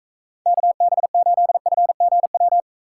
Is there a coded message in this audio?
Yes: Morse code from 0.5 to 2.6 s, heard as a keyed tone, about 44 wpm, 700 Hz. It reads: KB8FGW